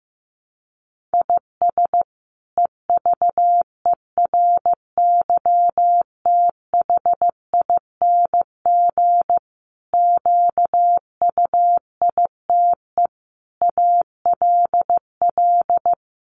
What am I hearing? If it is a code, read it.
Morse code, heard as a keyed tone, about 15 wpm, 700 Hz. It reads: IS EVERYTHING QUITE ALL
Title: Sentences from Top 700 Words 15wpm